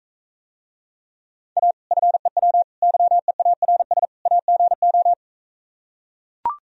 Morse code, heard as a keyed tone, about 43 wpm, 700 Hz. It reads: A FEW YEARS AGO